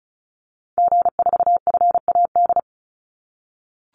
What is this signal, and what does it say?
Morse code, heard as a keyed tone, about 34 wpm, 700 Hz. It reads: G4FAD
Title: Call Signs - 35wpm